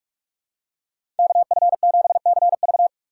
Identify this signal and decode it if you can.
Morse code, heard as a keyed tone, about 46 wpm, 700 Hz. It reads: KF7CV
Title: USA Call Signs - 5 Characters - 4x Character Spacing - 45wpm